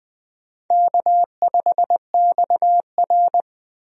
Morse code, heard as a keyed tone, about 20 wpm, 700 Hz. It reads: K5XR
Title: USA Call Signs - 4 Characters - 20wpm